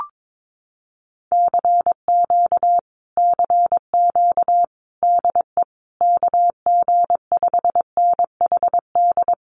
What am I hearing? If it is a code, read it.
Morse code, heard as a keyed tone, about 22 wpm, 700 Hz. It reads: CQ CQ DE KG5NHD
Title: Common QSO Phrases Encore 22wpm